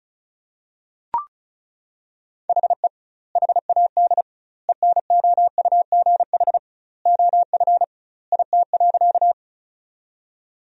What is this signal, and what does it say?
Morse code, heard as a keyed tone, about 34 wpm, 700 Hz. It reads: HE HAD ENOUGH OF IT.
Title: 2x Word-Spacing - Sentences from Top 400 Words - 35wpm